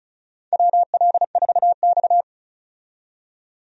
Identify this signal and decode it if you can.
Morse code, heard as a keyed tone, about 34 wpm, 700 Hz. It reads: WL4X
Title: USA Call Signs - 4 Characters - 35wpm